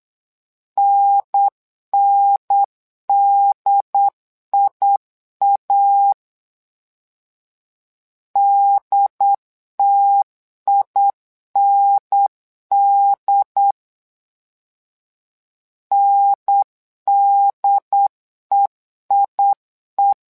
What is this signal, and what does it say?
Morse code, heard as a keyed tone, about 8 wpm, 800 Hz. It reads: NNDIA DTIND NDEIE
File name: LEZIONI / 2 UHDN / 2 UHDN 8wpm2232